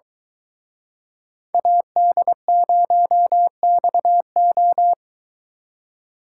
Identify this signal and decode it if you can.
Morse code, heard as a keyed tone, about 23 wpm, 700 Hz. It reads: AD0XO